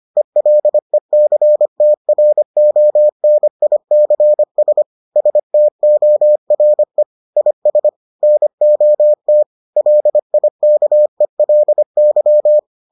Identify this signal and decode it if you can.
Morse code, heard as a keyed tone, about 25 wpm, 600 Hz. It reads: ELECTRONICS STORE IS NOT LIKELY